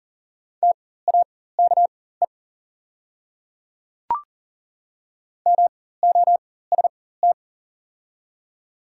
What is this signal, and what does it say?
Morse code, heard as a keyed tone, about 37 wpm, spaced as for 18 wpm, 700 Hz. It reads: TAKE MOST
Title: 4x Character Spacing - Top 200 Words - 40wpm